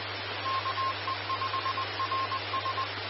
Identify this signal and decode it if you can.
Morse code, heard as a keyed tone, about 20 wpm, 1060 Hz. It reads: KE5RS
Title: Austin, TX Amateur Repeaters